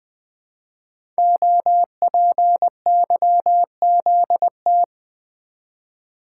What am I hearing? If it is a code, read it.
Morse code, heard as a keyed tone, about 20 wpm, 700 Hz. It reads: OPYZT
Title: Code Groups - 5 Letters - 20wpm